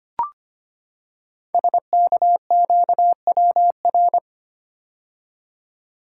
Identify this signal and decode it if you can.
Morse code, heard as a keyed tone, about 25 wpm, 700 Hz. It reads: SKQWR
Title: Code Groups - 5 Letter:Numbers - 25wpm